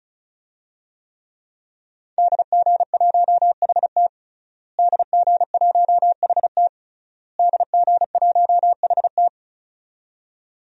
Morse code, heard as a keyed tone, about 36 wpm, 700 Hz. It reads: DG1HT DG1HT DG1HT
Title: DXCC-100 Call Signs - 5 Characters - Repeated 3x - 35wpm